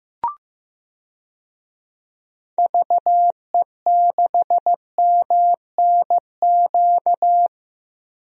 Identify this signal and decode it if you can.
Morse code, heard as a keyed tone, about 15 wpm, 700 Hz. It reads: VE6MNQ
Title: DXCC-100 Call Signs - 6 Characters - 15wpm